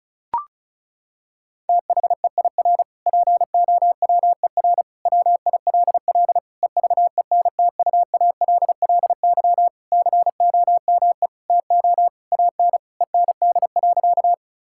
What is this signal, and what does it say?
Morse code, heard as a keyed tone, about 34 wpm, 700 Hz. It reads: THEIR POWER WILL EVENTUALLY COME TO AN END.
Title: Sentences from Top 1000 Words - Review Part 6 - 35wpm